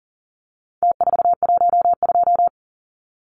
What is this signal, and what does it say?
Morse code, heard as a keyed tone, about 40 wpm, 700 Hz. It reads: T412
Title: Numbers - Contest Style - 4-Digit 40wpm